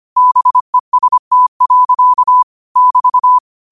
Morse code, heard as a keyed tone, about 25 wpm, 1000 Hz. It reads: BEST. <BT>